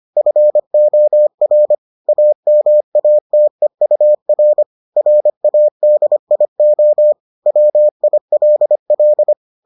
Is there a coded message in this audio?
Yes: Morse code from 0.2 to 9.3 s, heard as a keyed tone, about 25 wpm, 600 Hz. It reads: FOR AMATEUR RADIO WILL